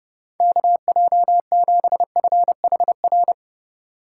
Morse code, heard as a keyed tone, about 31 wpm, 700 Hz. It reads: KJ7FHR
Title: USA Call Signs - 6 Characters - 2x Character Spacing - 30wpm